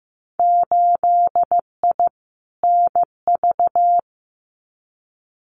Morse code, heard as a keyed tone, about 15 wpm, 700 Hz. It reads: 8I NV
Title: Winter Field Day Exchanges Encore 15wpm